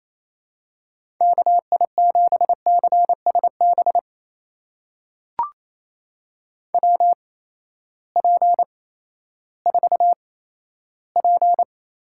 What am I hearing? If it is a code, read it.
Morse code, heard as a keyed tone, about 28 wpm, spaced as for 14 wpm, 700 Hz. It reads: KI7CSB WP4P
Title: USA Call Signs - 6 Characters - 8x Character Spacing - 28wpm